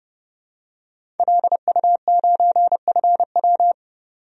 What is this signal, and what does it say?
Morse code, heard as a keyed tone, about 31 wpm, 700 Hz. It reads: LU9FW
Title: DXCC-100 Call Signs - 5 Characters - 30wpm